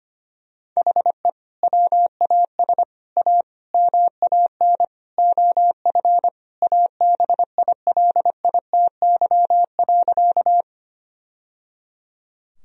Morse code, heard as a keyed tone, about 25 wpm, 700 Hz. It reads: HE WAS A MAN OF ABILITY.
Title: Sentences from Top 1000 Words - Review Part 3 - 25wpm